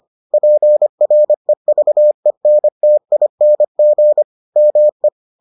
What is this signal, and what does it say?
Morse code, heard as a keyed tone, about 25 wpm, 600 Hz. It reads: PREVENTING ME